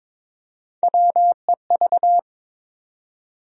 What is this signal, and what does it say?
Morse code, heard as a keyed tone, about 22 wpm, 700 Hz. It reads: WEV
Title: USA Call Sign Postfixes - 3 Letters 22wpm